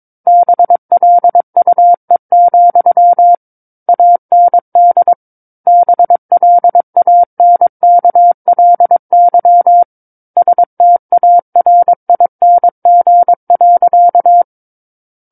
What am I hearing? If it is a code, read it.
Morse code, heard as a keyed tone, about 22 wpm, 700 Hz. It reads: BLUE, AND BLANKLY STARING.
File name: Book One / 13-Chapter 13 - 22 WPM